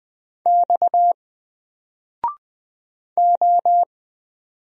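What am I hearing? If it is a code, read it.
Morse code, heard as a keyed tone, about 20 wpm, 700 Hz. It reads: X O
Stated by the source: Single letter and number Morse drill at 20 wpm, from N4LCD.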